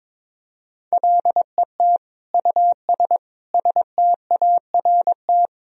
Morse code, heard as a keyed tone, about 22 wpm, 700 Hz. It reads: LET US START
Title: Sentences from Top 1000 Words - Review Part 1 - 22wpm